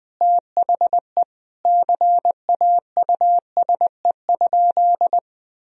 Morse code, heard as a keyed tone, about 21 wpm, 700 Hz. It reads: THE CAUSE?
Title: Sentences from Top 1000 Words - Review Part 1 - 20wpm